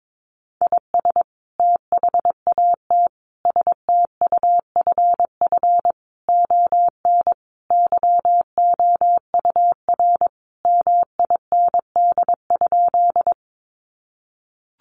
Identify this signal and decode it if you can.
Morse code, heard as a keyed tone, about 22 wpm, 700 Hz. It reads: IS THAT STUFF ON YOUR MIND?
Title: Sentences from Top 1400 Words 22wpm